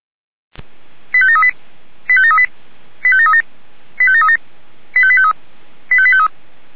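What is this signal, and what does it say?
Five-tone selective call (ZVEI), heard as quick runs of five tones, tones of about 80 ms. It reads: 85738 85738 85737 85737 85853 85853
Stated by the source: Five-tone ZVEI selective call received on an SDR.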